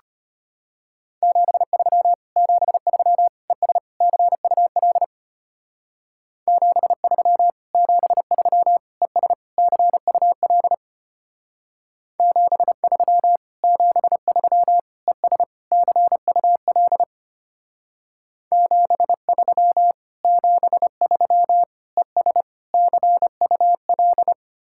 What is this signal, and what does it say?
Morse code, heard as a keyed tone, about 31 wpm, 700 Hz. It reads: 73 73 ES CUL 73 73 ES CUL 73 73 ES CUL 73 73 ES CUL
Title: Speed-Racing Common QSO Phrases Encore - 25wpm